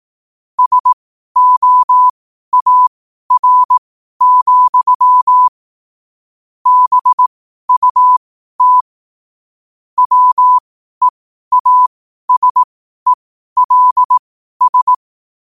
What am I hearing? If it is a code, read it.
Morse code, heard as a keyed tone, about 18 wpm, spaced as for 12 wpm, 1000 Hz. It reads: SOAR, BUT WEASELS
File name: qotdfiles / 12wpm